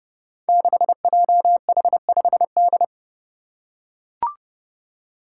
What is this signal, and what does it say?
Morse code, heard as a keyed tone, about 30 wpm, 700 Hz. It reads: 6JH5D
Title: Five Challenging Letter-Number Combinations 30wpm